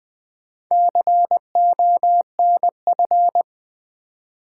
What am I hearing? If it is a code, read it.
Morse code, heard as a keyed tone, about 20 wpm, 700 Hz. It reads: CONF
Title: Build A Word - 4 Letters At A Time - 12-Letter Words - 20wpm